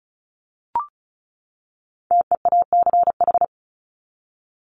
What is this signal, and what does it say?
Morse code, heard as a keyed tone, about 34 wpm, 700 Hz. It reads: TEACH
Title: ING Suffix - Top 100 Words with Base Words 35wpm